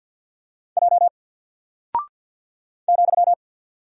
Morse code, heard as a keyed tone, about 48 wpm, 700 Hz. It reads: J ,
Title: Single Character (Letter:Number:Punctuation:Prosign) Alt 50wpm